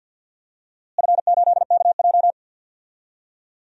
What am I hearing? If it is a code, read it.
Morse code, heard as a keyed tone, about 45 wpm, 700 Hz. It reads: F8KJ
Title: DXCC-100 Call Signs - 4 Characters - Repeated 2x - 50wpm